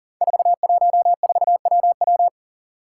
Morse code, heard as a keyed tone, about 39 wpm, 700 Hz. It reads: 414WW
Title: Five Challenging Letter-Number Combinations 40wpm